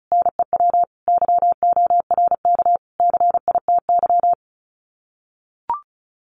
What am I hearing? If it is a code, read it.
Morse code, heard as a keyed tone, about 34 wpm, 700 Hz. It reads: NEW YORK CITY
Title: Sets of 5 Words Encore2 35wpm